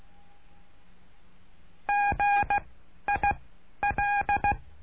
Morse code, heard as a keyed tone, about 16 wpm, spaced as for 11 wpm, 810 Hz. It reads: GIL
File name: cursos / CURSO DE TELEGRAFIA / Telegrafia - Trilha 09